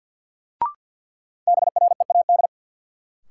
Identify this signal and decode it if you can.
Morse code, heard as a keyed tone, about 52 wpm, 700 Hz. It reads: BREAD
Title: Sentences from Top 1600 Words - New Word Drill - 50wpm